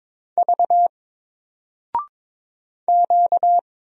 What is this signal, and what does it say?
Morse code, heard as a keyed tone, about 23 wpm, 700 Hz. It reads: V Q
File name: Single-Letter-Number-22wpm